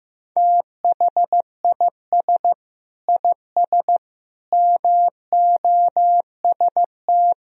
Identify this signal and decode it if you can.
Morse code, heard as a keyed tone, about 15 wpm, 700 Hz. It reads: THIS IS MOST
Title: Sentences from Top 1000 Words - Review Part 1 - 15wpm